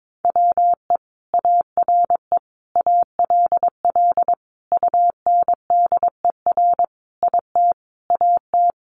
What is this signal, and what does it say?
Morse code, heard as a keyed tone, about 22 wpm, 700 Hz. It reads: WE ARE ALL UNDER IT AT
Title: Sentences from Top 1500 Words 22wpm